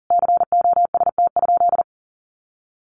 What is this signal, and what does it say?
Morse code, heard as a keyed tone, about 37 wpm, 700 Hz. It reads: COST?
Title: Sentences from Top 1400 Words 40wpm